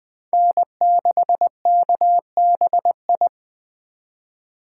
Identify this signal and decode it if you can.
Morse code, heard as a keyed tone, about 23 wpm, 700 Hz. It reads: N6KBI
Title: Speed Racing - USA 1x3 Call Signs 20wpm